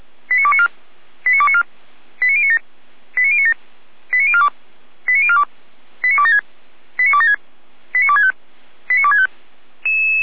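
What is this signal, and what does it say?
Five-tone selective call (ZVEI), heard as quick runs of five tones, tones of about 70 ms. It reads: 89294 89294 89097 89097 89042 89042 89276 89276 89265 89265
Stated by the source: ZVEI five-tone selective call received via SDR, fire brigade Feuerwehr Orlamünde.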